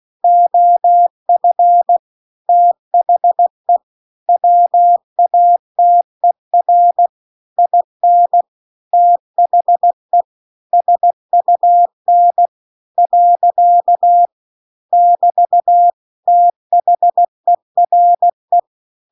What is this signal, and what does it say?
Morse code, heard as a keyed tone, about 16 wpm, 700 Hz. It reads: OF THE WATER IN THE SUN . <BT> THERE